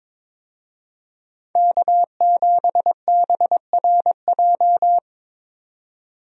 Morse code, heard as a keyed tone, about 22 wpm, 700 Hz. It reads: K7BRJ
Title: USA Call Signs - 5 Characters - 22wpm